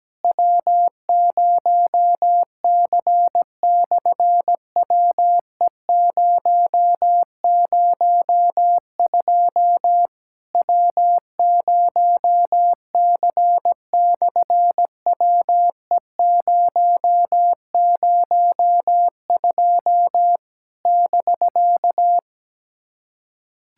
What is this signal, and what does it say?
Morse code, heard as a keyed tone, about 17 wpm, 700 Hz. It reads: W0C/WE002 W0C/WE002 <BK>
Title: SOTA QSO Practice 17wpm